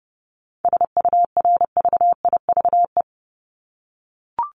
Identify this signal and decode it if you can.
Morse code, heard as a keyed tone, about 30 wpm, 700 Hz. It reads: SURVIVE
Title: Top 1251-1500 Words 30wpm